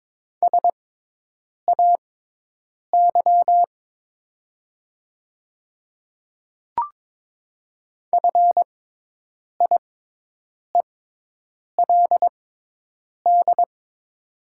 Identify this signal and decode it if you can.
Morse code, heard as a keyed tone, about 22 wpm, spaced as for 8 wpm, 700 Hz. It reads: SAY FIELD
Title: 6x Character Spacing - Top 500 Words - 22wpm